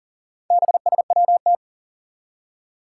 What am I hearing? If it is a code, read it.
Morse code, heard as a keyed tone, about 42 wpm, 700 Hz. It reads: BSWT
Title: Four Challenging Letter-Number Combinations 40wpm